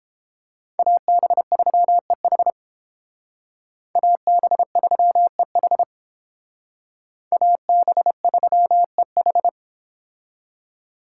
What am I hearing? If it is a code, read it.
Morse code, heard as a keyed tone, about 31 wpm, 700 Hz. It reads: AB3EH AB3EH AB3EH